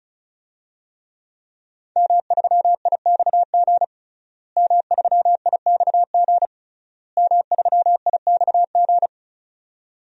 Morse code, heard as a keyed tone, about 36 wpm, 700 Hz. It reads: M3IXG M3IXG M3IXG
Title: DXCC-100 Call Signs - 5 Characters - Repeated 3x - 35wpm